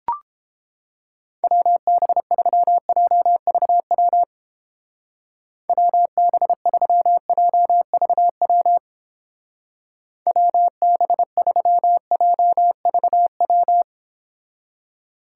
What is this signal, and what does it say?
Morse code, heard as a keyed tone, about 29 wpm, 700 Hz. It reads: WB3JVW WB3JVW WB3JVW